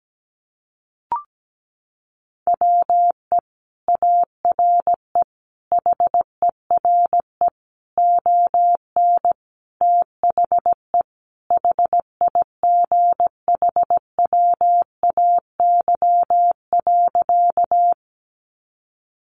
Morse code, heard as a keyed tone, about 17 wpm, 700 Hz. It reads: WE ARE HERE ON THE HIGHWAY.
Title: Sentences from Top 1400 Words 17wpm